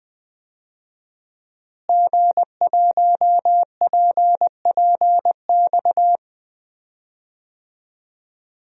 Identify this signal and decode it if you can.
Morse code, heard as a keyed tone, about 21 wpm, 700 Hz. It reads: G1PPX
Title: DXCC-100 Call Signs - 5 Characters - Repeated 2x - 20wpm